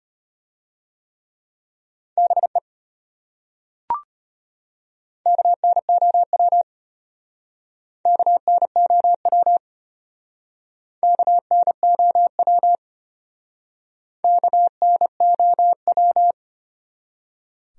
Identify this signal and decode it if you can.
Morse code, heard as a keyed tone, about 31 wpm, 700 Hz. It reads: BE KNOW KNOW KNOW KNOW
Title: Speed-Racing Top 200 Words 25wpm